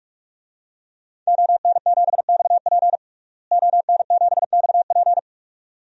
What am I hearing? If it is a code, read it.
Morse code, heard as a keyed tone, about 46 wpm, 700 Hz. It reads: ON7XP ON7XP
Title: DXCC-100 Call Signs - 5 Characters - Repeated 2x - 45wpm